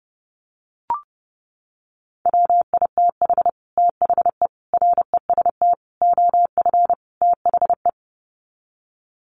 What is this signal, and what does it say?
Morse code, heard as a keyed tone, about 30 wpm, 700 Hz. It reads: WITH THE REST OF THE
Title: Sets of 5 Words 30wpm